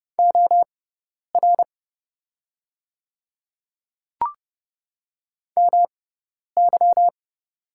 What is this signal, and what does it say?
Morse code, heard as a keyed tone, about 31 wpm, spaced as for 10 wpm, 700 Hz. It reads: OR MY
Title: 6x Character Spacing - Top 500 Words - 30wpm